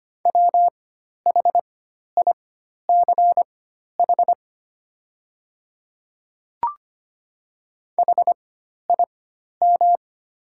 Morse code, heard as a keyed tone, about 25 wpm, spaced as for 12 wpm, 700 Hz. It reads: WHICH HIM
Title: 4x Character Spacing - Top 100 Words - 25wpm